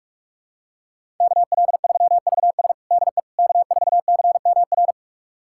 Morse code, heard as a keyed tone, about 46 wpm, 700 Hz. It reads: KL3VS DE K4CMR
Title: USA Callsign DE Callsign - 5 Characters - 45wpm